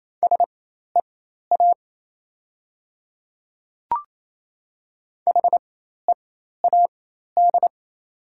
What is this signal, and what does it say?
Morse code, heard as a keyed tone, about 28 wpm, spaced as for 13 wpm, 700 Hz. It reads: SEA HEAD
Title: 4x Character Spacing - Top 500 Words - 28wpm